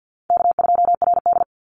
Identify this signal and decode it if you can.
Morse code, heard as a keyed tone, about 50 wpm, 700 Hz. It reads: K2RD
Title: Easy Call Signs Repeated 3x - 50wpm